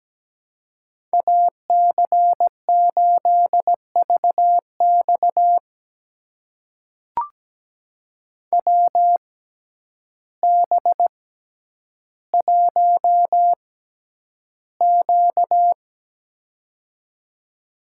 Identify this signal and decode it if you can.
Morse code, heard as a keyed tone, about 17 wpm, spaced as for 9 wpm, 700 Hz. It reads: AC8VX WB1Q
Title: USA Call Signs - 6x Character Spacing - 17wpm